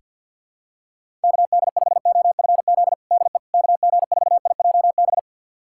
Morse code, heard as a keyed tone, about 47 wpm, 700 Hz. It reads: KD5OFZ DE KG4IJB
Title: USA Callsign DE Callsign - 6 Characters - 50wpm